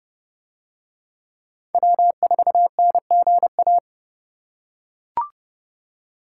Morse code, heard as a keyed tone, about 30 wpm, 700 Hz. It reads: W4NGA